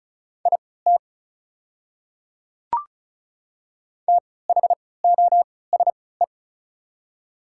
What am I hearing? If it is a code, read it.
Morse code, heard as a keyed tone, about 36 wpm, spaced as for 20 wpm, 700 Hz. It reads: IT THOSE
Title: 3x Character Spacing - Top 300 Words - 35wpm